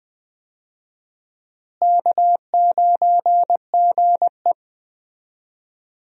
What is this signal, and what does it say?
Morse code, heard as a keyed tone, about 21 wpm, 700 Hz. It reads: K9GE
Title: USA Call Signs - 4 Characters - Repeated 3x - 20wpm